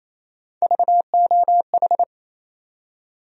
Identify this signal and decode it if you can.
Morse code, heard as a keyed tone, about 28 wpm, 700 Hz. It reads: VOH
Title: USA Call Sign Postfixes - 3 Letters 28wpm